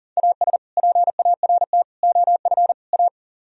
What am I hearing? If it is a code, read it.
Morse code, heard as a keyed tone, about 50 wpm, 700 Hz. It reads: AS PART OF A
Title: Speed-Racing - Sets of 4 Words 40wpm